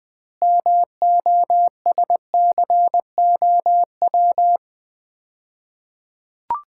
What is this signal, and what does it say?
Morse code, heard as a keyed tone, about 20 wpm, 700 Hz. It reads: MOSCOW
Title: Common QSO Phrases - 2x Word-Spacing - 20wpm